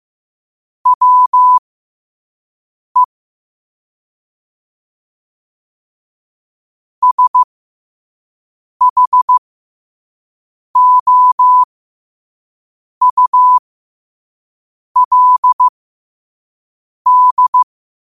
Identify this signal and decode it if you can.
Morse code, heard as a keyed tone, about 15 wpm, spaced as for 5 wpm, 1000 Hz. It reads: WE SHOULD
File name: qotdfiles / 5wpm